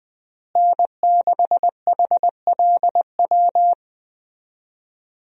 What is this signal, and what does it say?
Morse code, heard as a keyed tone, about 20 wpm, 700 Hz. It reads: N6HLW
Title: USA Call Signs - 5 Characters - Repeated 3x - 20wpm